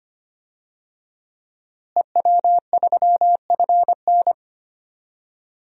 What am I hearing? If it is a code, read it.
Morse code, heard as a keyed tone, about 25 wpm, 700 Hz. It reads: EW3FN
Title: DXCC-100 Call Signs - 5 Characters - Repeated 2x - 25wpm